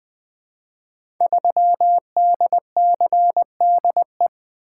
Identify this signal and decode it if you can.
Morse code, heard as a keyed tone, about 21 wpm, 700 Hz. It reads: 3DCDE
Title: Code Groups - 5 Letter:Numbers - 20wpm